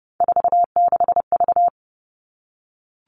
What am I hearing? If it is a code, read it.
Morse code, heard as a keyed tone, about 28 wpm, 700 Hz. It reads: 46V
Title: All Cut Numbers - 3-Digit 30wpm